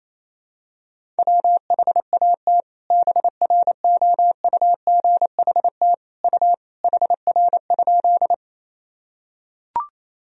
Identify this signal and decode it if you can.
Morse code, heard as a keyed tone, about 28 wpm, 700 Hz. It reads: WHAT BROUGHT U HR?
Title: Sentences from Top 500 Words and Abbreviations 28wpm